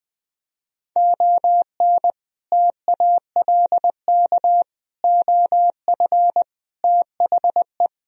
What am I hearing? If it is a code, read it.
Morse code, heard as a keyed tone, about 20 wpm, 700 Hz. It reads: ON TALK OF THE
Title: Speed-Racing - Sets of 5 Words 20wpm